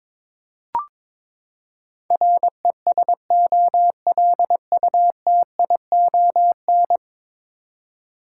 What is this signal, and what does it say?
Morse code, heard as a keyed tone, about 22 wpm, 700 Hz. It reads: RESOLUTION
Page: morse-code-ninja-practice-22wpm